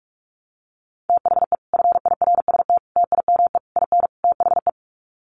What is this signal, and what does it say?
Morse code, heard as a keyed tone, about 44 wpm, 700 Hz. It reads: THE FIRST TIME IN THE
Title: Sets of 5 Words Encore 45wpm